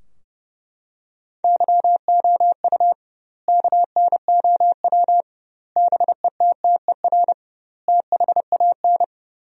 Morse code, heard as a keyed tone, about 30 wpm, 700 Hz. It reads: YOU KNOW BETTER THAN
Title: 2x Word-Spacing - Sentences from Top 400 Words - 30wpm